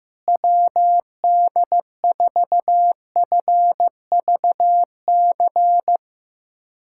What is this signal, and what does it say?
Morse code, heard as a keyed tone, about 15 wpm, 700 Hz. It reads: WD4FVC
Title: USA Callsign DE Callsign 15wpm